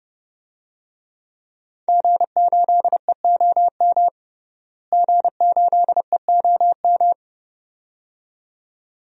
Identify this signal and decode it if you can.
Morse code, heard as a keyed tone, about 30 wpm, 700 Hz. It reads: G8EOM G8EOM
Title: DXCC-100 Call Signs - 5 Characters - Repeated 2x - 30wpm